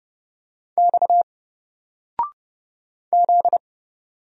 Morse code, heard as a keyed tone, about 30 wpm, 700 Hz. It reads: X Z
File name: Single-Letter-Number-30wpm